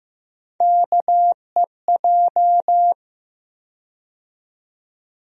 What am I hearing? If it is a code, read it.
Morse code, heard as a keyed tone, about 15 wpm, 700 Hz. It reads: KEJ
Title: USA Call Sign Postfixes - 3 Letters 15wpm